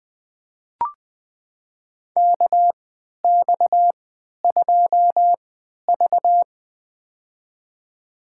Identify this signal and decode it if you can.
Morse code, heard as a keyed tone, about 20 wpm, spaced as for 11 wpm, 700 Hz. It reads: KX2V
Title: USA Call Signs - 4 Characters - 3x Character Spacing - 20wpm